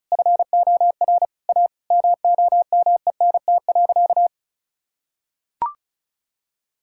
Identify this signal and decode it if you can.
Morse code, heard as a keyed tone, about 34 wpm, 700 Hz. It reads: FOR A MOMENT.
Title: Sentences from Top 1600 Words 35wpm